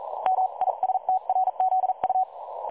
Morse code, heard as a keyed tone, about 43 wpm, spaced as for 37 wpm, 750 Hz. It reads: MISTRZU
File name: dlazaawansowanych / nagraniazpasma / websdr recording 3565.0kHz